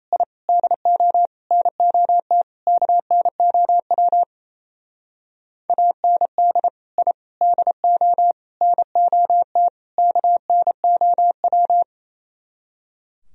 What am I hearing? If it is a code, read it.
Morse code, heard as a keyed tone, about 36 wpm, 700 Hz. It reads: I DO NOT KNOW AND I DO NOT KNOW